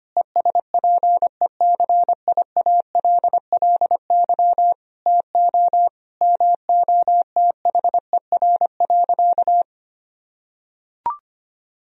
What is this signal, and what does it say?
Morse code, heard as a keyed tone, about 25 wpm, 700 Hz. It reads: ESPECIALLY TO MOTHER.
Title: Sentences from Top 1000 Words - Review Part 4 - 25wpm